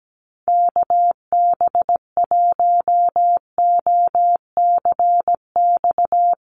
Morse code, heard as a keyed tone, about 17 wpm, 700 Hz. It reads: KB1OCX
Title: USA Call Signs Repeated 3x 17wpm